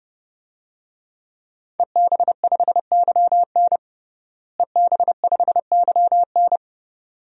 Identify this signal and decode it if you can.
Morse code, heard as a keyed tone, about 30 wpm, 700 Hz. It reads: EB5YN EB5YN
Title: DXCC-100 Call Signs - 5 Characters - Repeated 2x - 30wpm